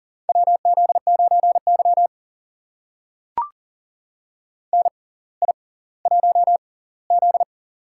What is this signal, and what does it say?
Morse code, heard as a keyed tone, about 40 wpm, spaced as for 20 wpm, 700 Hz. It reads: WZ9Y NI1Z
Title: USA Call Signs - 4 Characters - 6x Character Spacing - 40wpm